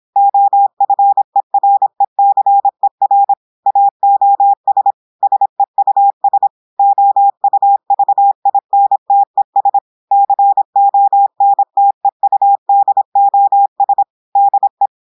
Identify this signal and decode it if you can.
Morse code, heard as a keyed tone, about 26 wpm, 800 Hz. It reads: OFERECER AOS SEUS OUVINTES CONTEUDOS DE
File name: portugais texte en clair 2 vitesse 26 mots minutes